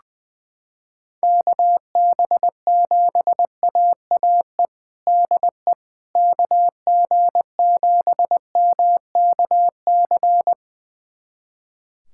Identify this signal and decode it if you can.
Morse code, heard as a keyed tone, about 21 wpm, 700 Hz. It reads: KB7AAE DE KG7MKC
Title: USA Callsign DE Callsign - 6 Characters - 20wpm